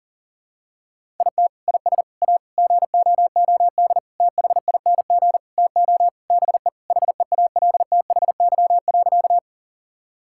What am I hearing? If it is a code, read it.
Morse code, heard as a keyed tone, about 37 wpm, 700 Hz. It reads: IT IS A GOOD THING TO BE HEALTHY.
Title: Sentences from Top 1000 Words - Review Part 5 - 40wpm